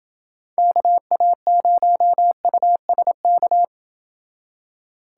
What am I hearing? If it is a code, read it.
Morse code, heard as a keyed tone, about 29 wpm, 700 Hz. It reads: KA0USK